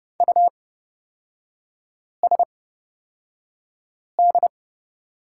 Morse code, heard as a keyed tone, about 29 wpm, spaced as for 5 wpm, 700 Hz. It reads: USD
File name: Lesson 38e - Letter q - Three Characters 30wpm